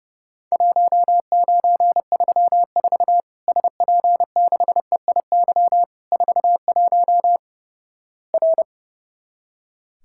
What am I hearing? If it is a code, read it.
Morse code, heard as a keyed tone, about 29 wpm, 700 Hz. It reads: 1934 SP6EIY 41 R
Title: Worked All Europe DX Contest - QTC Exchange - 30wpm